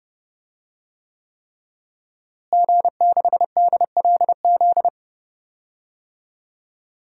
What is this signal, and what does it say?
Morse code, heard as a keyed tone, about 30 wpm, 700 Hz. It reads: G6DLZ
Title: DXCC-100 Call Signs - 5 Characters - Repeated 3x - 30wpm